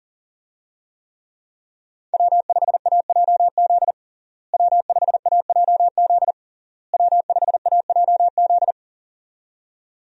Morse code, heard as a keyed tone, about 40 wpm, 700 Hz. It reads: W5AJZ W5AJZ W5AJZ
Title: USA Call Signs - 5 Characters - Repeated 3x - 40wpm